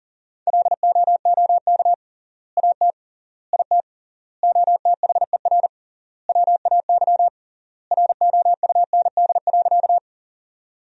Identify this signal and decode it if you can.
Morse code, heard as a keyed tone, about 40 wpm, 700 Hz. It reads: LOOK AT IT OTHER WAY ROUND.
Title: 3x Word-Spacing - Sentences from Top 200 Words - 40wpm